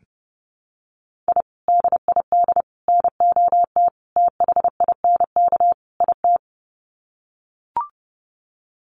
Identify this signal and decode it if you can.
Morse code, heard as a keyed tone, about 30 wpm, 700 Hz. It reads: I DID NOT THINK IT
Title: Sets of 5 Words Encore2 30wpm